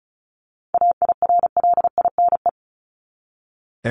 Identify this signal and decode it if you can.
Morse code, heard as a keyed tone, about 34 wpm, 700 Hz. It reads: AIRLINE
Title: Top 1501-1750 Words 35wpm